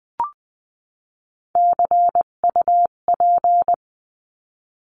Morse code, heard as a keyed tone, about 20 wpm, 700 Hz. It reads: CUP
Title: 258 Lesson 42c - Words 16-20wpm